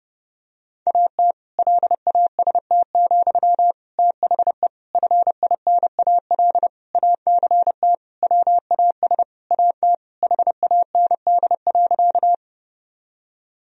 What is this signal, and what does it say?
Morse code, heard as a keyed tone, about 30 wpm, 700 Hz. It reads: AT LAST, THE FINAL ACT WAS AT HAND.
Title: Sentences from Top 1000 Words - Review Part 2 - 30wpm